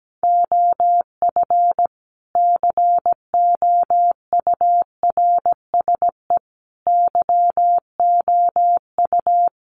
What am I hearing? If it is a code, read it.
Morse code, heard as a keyed tone, about 17 wpm, 700 Hz. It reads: OF COURSE YOU
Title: Sentences from Top 1300 Words 17wpm